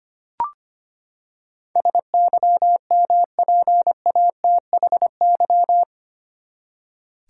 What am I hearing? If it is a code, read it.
Morse code, heard as a keyed tone, about 25 wpm, 700 Hz. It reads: SYMPATHY
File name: Eight Letter Words - 25wpm